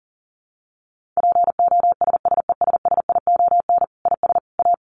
Morse code, heard as a keyed tone, about 40 wpm, 700 Hz. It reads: POSSESSION IS A
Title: Sentences from Top 1400 Words 40wpm